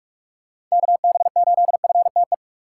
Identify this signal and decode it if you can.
Morse code, heard as a keyed tone, about 46 wpm, 700 Hz. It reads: KB8FTE
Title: USA Call Signs - 6 Characters - 2x Character Spacing - 45wpm